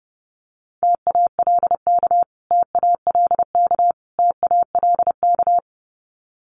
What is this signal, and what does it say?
Morse code, heard as a keyed tone, about 30 wpm, 700 Hz. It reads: TALK TALK TALK
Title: Top 500 Words Repeated 30wpm